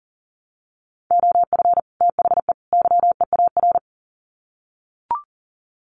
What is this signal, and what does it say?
Morse code, heard as a keyed tone, about 40 wpm, 700 Hz. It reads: OF THE YEAR
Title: Sets of 3 Words 40wpm